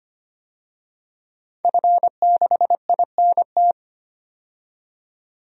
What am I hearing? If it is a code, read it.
Morse code, heard as a keyed tone, about 25 wpm, 700 Hz. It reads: F6INT
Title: DXCC-100 Call Signs - 5 Characters - 25wpm